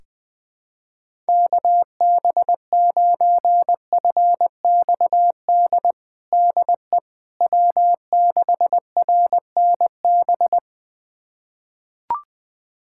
Morse code, heard as a keyed tone, about 21 wpm, 700 Hz. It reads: KB9FXD DE W6RNB
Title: USA Callsign DE Callsign 20wpm